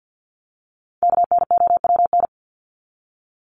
Morse code, heard as a keyed tone, about 50 wpm, 700 Hz. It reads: KNOWN
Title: Top 501-750 Words 50wpm